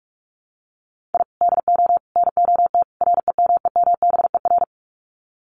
Morse code, heard as a keyed tone, about 44 wpm, 700 Hz. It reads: I DO NOT REMEMBER
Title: Sets of 4 Words 45wpm